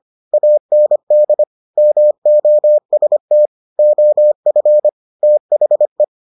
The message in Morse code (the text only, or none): AND MOST OF THE